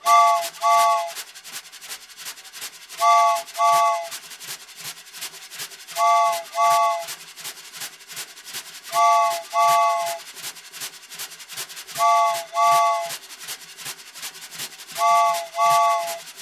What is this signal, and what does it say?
There is no code.